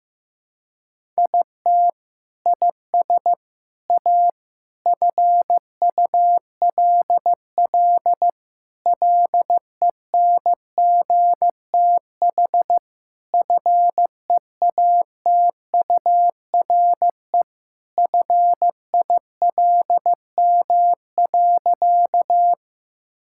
Morse code, heard as a keyed tone, about 14 wpm, 700 Hz. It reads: IT IS A FULL LENGTH FEATURE FILM.
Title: Sentences from Top 1000 Words - Review Part 5 - 15wpm